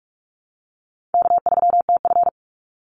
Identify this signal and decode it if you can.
Morse code, heard as a keyed tone, about 44 wpm, 700 Hz. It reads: K3TF
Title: Easy Call Signs Repeated 3x - 45wpm